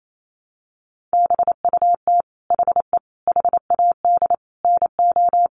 Morse code, heard as a keyed tone, about 28 wpm, 700 Hz. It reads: BUT HE HAD NO
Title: Sentences from Top 100 Words 28wpm